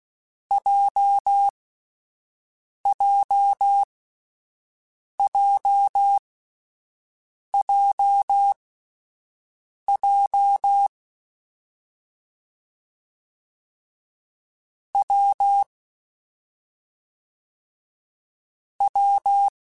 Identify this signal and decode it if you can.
Morse code, heard as a keyed tone, about 16 wpm, spaced as for 5 wpm, 800 Hz. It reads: JJJJJ W W
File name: Difficult Combos / Difficult Combos-mp3s / WPJ WPJ1 @5